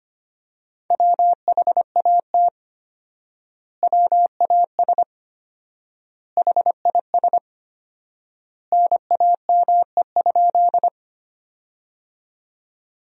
Morse code, heard as a keyed tone, about 25 wpm, 700 Hz. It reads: WHAT WAS HIS NAME?
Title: 4x Word-Spacing - Sentences from Top 200 Words - 25wpm